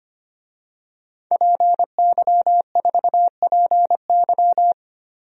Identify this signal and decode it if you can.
Morse code, heard as a keyed tone, about 25 wpm, 700 Hz. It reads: PY4PY
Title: DXCC-100 Call Signs - 5 Characters - 25wpm